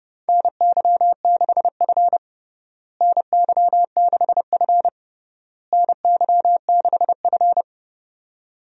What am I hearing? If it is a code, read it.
Morse code, heard as a keyed tone, about 29 wpm, 700 Hz. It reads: NY6F NY6F NY6F
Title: USA Call Signs - 4 Characters - Repeated 3x - 30wpm